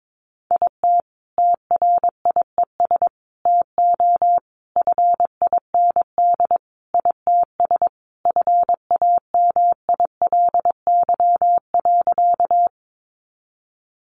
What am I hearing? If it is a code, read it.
Morse code, heard as a keyed tone, about 22 wpm, 700 Hz. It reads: IT TRIES TO FIND ITS FAMILY.
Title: Additional Easy Phrases 22wpm